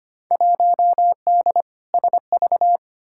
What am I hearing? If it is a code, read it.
Morse code, heard as a keyed tone, about 25 wpm, 700 Hz. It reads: 1D SV
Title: Field Day Exchanges - 25wpm